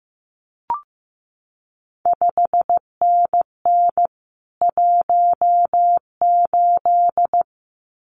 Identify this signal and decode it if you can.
Morse code, heard as a keyed tone, about 15 wpm, 700 Hz. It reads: 5NN 18
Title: CQ WW DX Contest Exchange Encore - 15wpm